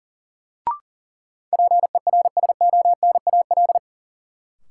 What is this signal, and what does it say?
Morse code, heard as a keyed tone, about 40 wpm, 700 Hz. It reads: PERSONAL
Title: Sentences from Top 700 Words - New Word Drill - 40wpm